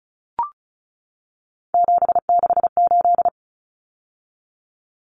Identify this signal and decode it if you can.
Morse code, heard as a keyed tone, about 34 wpm, 700 Hz. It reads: 768
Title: Common Cut Numbers - 3-Digit 35wpm